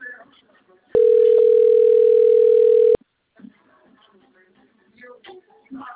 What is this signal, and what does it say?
Ringback tone (North America), 440 + 480 Hz, with speech.
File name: rg-1600-unknown- - - .71986